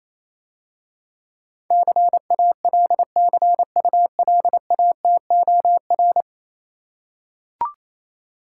Morse code, heard as a keyed tone, about 29 wpm, 700 Hz. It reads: CALCULATOR